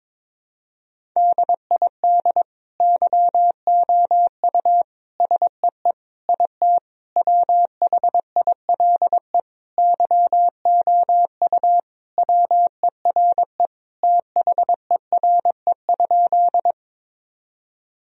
Morse code, heard as a keyed tone, about 22 wpm, 700 Hz. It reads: DID YOU SEE IT WHILE YOU WERE THERE?
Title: Sentences from Top 1000 Words - Review Part 1 - 22wpm